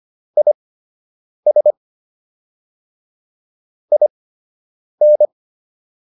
Morse code, heard as a keyed tone, about 25 wpm, spaced as for 8 wpm, 600 Hz. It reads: IS IN